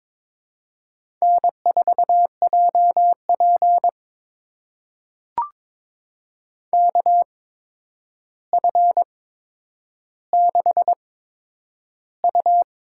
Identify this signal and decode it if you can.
Morse code, heard as a keyed tone, about 22 wpm, spaced as for 9 wpm, 700 Hz. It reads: N4JP KF6U
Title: USA Call Signs - 4 Characters - 8x Character Spacing - 22wpm